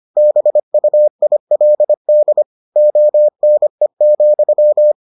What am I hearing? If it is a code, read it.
Morse code, heard as a keyed tone, about 25 wpm, 600 Hz. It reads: BUILD ONE,